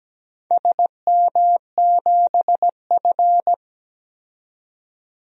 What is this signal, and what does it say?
Morse code, heard as a keyed tone, about 17 wpm, 700 Hz. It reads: SM7F
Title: DXCC-100 Call Signs - 4 Characters - 17wpm